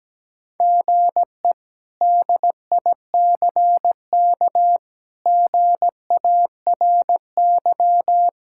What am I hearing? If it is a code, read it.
Morse code, heard as a keyed tone, about 17 wpm, 700 Hz. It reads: GE DICK GARY
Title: K1USN SST QSO Practice - 17wpm